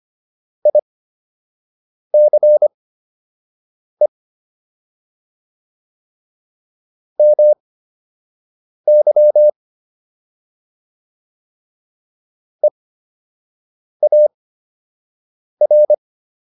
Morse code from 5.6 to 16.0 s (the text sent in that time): MY EAR